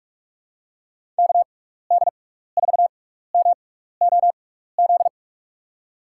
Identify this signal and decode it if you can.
Morse code, heard as a keyed tone, about 46 wpm, spaced as for 15 wpm, 700 Hz. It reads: KD4MOZ
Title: USA Call Signs - 6 Characters - 6x Character Spacing - 45wpm